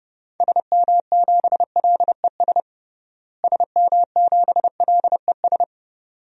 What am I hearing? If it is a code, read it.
Morse code, heard as a keyed tone, about 32 wpm, 700 Hz. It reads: SM7LES SM7LES
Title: DXCC-100 Call Signs - 6 Characters - Repeated 2x - 30wpm